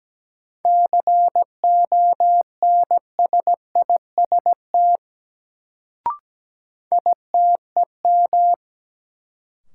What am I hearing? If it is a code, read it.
Morse code, heard as a keyed tone, about 17 wpm, 700 Hz. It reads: CONSIST ITEM
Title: Sentences from Top 1400 Words - New Word Drill Alt - 17wpm